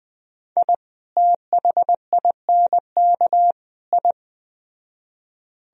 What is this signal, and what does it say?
Morse code, heard as a keyed tone, about 20 wpm, 700 Hz. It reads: I THINK I
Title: Speed-Racing - Sets of 3 Words 17wpm